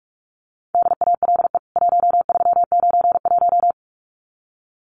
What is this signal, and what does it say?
Morse code, heard as a keyed tone, about 44 wpm, 700 Hz. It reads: DALE 1391
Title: CWT Exchange 45wpm